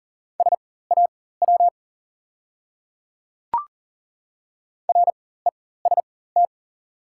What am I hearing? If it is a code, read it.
Morse code, heard as a keyed tone, about 40 wpm, spaced as for 19 wpm, 700 Hz. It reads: SAW REST
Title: 4x Character Spacing - Top 500 Words - 40wpm